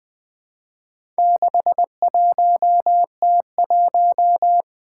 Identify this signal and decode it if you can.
Morse code, heard as a keyed tone, about 21 wpm, 700 Hz. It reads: 61T1
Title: Four Letter-Number Combinations - 20wpm